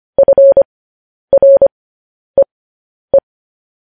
Morse code, heard as a keyed tone, about 25 wpm, spaced as for 10 wpm, 550 Hz. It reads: FREE
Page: news-headlines-in-morse-code-at-10-wpm